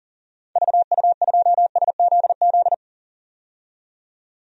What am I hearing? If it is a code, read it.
Morse code, heard as a keyed tone, about 40 wpm, 700 Hz. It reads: VU2SZZ
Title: DXCC-100 Call Signs - 6 Characters - 40wpm